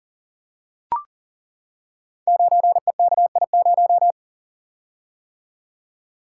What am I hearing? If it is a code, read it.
Morse code, heard as a keyed tone, about 39 wpm, 700 Hz. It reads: 9EKI0
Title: Code Groups - 5 Letter:Numbers - 40wpm